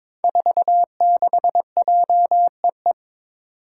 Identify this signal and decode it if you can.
Morse code, heard as a keyed tone, about 23 wpm, 700 Hz. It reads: 46JEE
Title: Five Challenging Letter-Number Combinations 22wpm